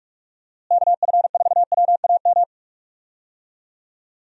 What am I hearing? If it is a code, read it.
Morse code, heard as a keyed tone, about 43 wpm, 700 Hz. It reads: KF4WAM
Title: USA Call Signs - 6 Characters - Repeated 2x - 45wpm